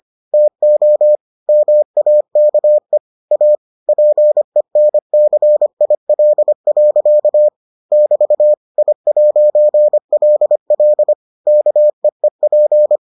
Morse code, heard as a keyed tone, about 25 wpm, 600 Hz. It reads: TO MAKE A PENCIL. <BT> I'LL KEEP